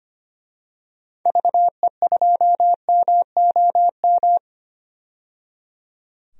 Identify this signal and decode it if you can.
Morse code, heard as a keyed tone, about 25 wpm, 700 Hz. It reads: VE2MOM
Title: DXCC-100 Call Signs - 6 Characters - 25wpm